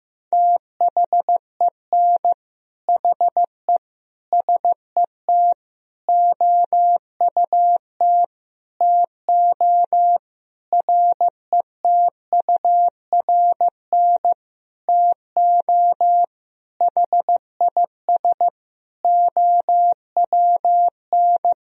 Morse code, heard as a keyed tone, about 15 wpm, 700 Hz. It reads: THEN HE SET OUT TO RETURN TO HIS OWN
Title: Sentences from Top 1000 Words - Review Part 1 - 15wpm